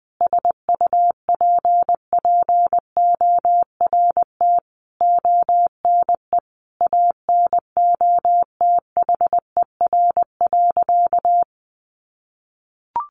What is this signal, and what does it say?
Morse code, heard as a keyed tone, about 20 wpm, 700 Hz. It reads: SUPPORT ONE ANOTHER.
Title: Sentences from Top 1400 Words 20wpm